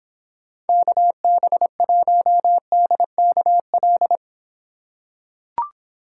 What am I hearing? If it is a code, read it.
Morse code, heard as a keyed tone, about 26 wpm, 700 Hz. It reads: KB1DKL